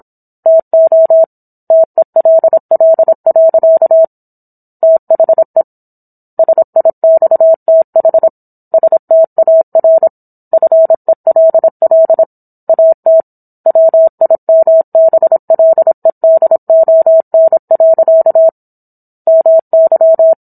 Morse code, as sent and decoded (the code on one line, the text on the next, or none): - --- / - . .-.. .-.. .-.-.- / - .... / ... .. -..- - .... / ... - .- .-. / ..-. .-.. .-.. / .- - / .-- .. -- -... .-.. -.. --- -. .-.-.- / -- -.--
TO TELL. THE SIXTH STAR FELL AT WIMBLEDON. MY